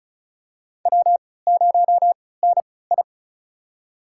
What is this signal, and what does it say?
Morse code, heard as a keyed tone, about 34 wpm, spaced as for 20 wpm, 700 Hz. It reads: W0NI
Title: USA Call Signs - 4 Characters - 3x Character Spacing - 35wpm